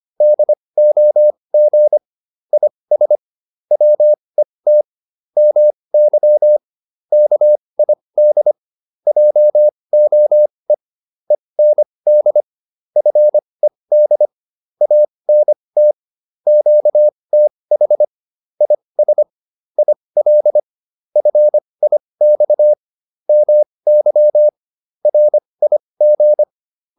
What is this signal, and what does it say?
Morse code, heard as a keyed tone, about 25 wpm, spaced as for 20 wpm, 600 Hz. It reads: DOG IS WET MY KID JOE END FED ANT QTH IS IL FIX MY RIG